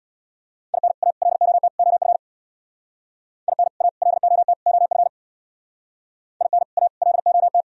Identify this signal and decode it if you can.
Morse code, heard as a keyed tone, about 60 wpm, 700 Hz. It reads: IN A LOT OF IN A LOT OF IN A LOT
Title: Speed-Racing - Sets of 4 Words 50wpm